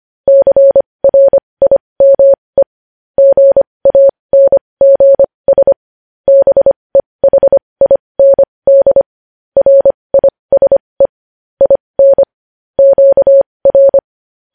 Morse code, heard as a keyed tone, about 25 wpm, spaced as for 20 wpm, 550 Hz. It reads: CRIME GANGS BEHIND RISE IN QR
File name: podcast / News headlines-1x-25x20